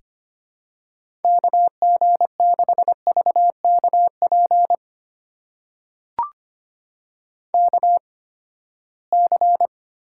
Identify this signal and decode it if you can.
Morse code, heard as a keyed tone, about 25 wpm, spaced as for 17 wpm, 700 Hz. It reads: KG6VKP KC
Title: USA Call Signs - 6 Characters - 8x Character Spacing - 25wpm